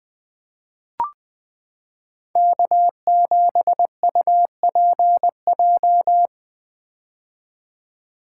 Morse code, heard as a keyed tone, about 21 wpm, 700 Hz. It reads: K7UPJ
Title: K1USN SST QSO Practice - 20wpm